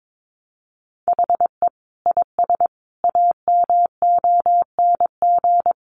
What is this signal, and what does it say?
Morse code, heard as a keyed tone, about 22 wpm, 700 Hz. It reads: HE IS AMONG
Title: Sentences from Top 500 Words 22wpm